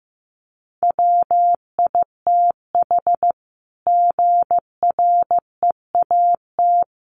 Morse code, heard as a keyed tone, about 15 wpm, 700 Hz. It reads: WITH GREAT
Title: Sentences from Top 900 Words 15wpm